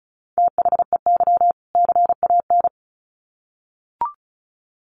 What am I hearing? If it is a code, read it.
Morse code, heard as a keyed tone, about 34 wpm, 700 Hz. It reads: THEY CAN
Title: Sets of 2 Words 35wpm